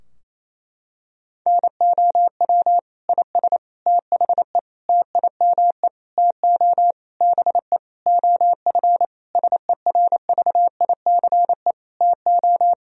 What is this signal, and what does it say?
Morse code, heard as a keyed tone, about 28 wpm, 700 Hz. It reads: NOW IS THE TIME TO BE OF SERVICE TO
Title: Sentences from Top 1000 Words - Review Part 4 - 28wpm